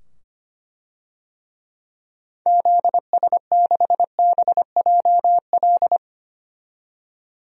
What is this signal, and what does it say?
Morse code, heard as a keyed tone, about 25 wpm, 700 Hz. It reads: ZS6BJL
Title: DXCC-100 Call Signs - 6 Characters - 25wpm